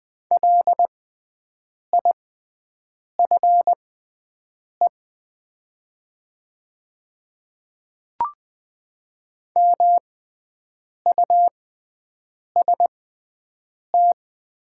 Morse code, heard as a keyed tone, about 20 wpm, spaced as for 7 wpm, 690 Hz. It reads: LIFE MUST
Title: 6x Character Spacing - Top 300 Words - 20wpm